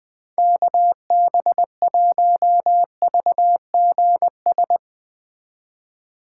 Morse code, heard as a keyed tone, about 20 wpm, 700 Hz. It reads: KB1VGS
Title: Speed Racing - USA 2x3 Call Signs 17wpm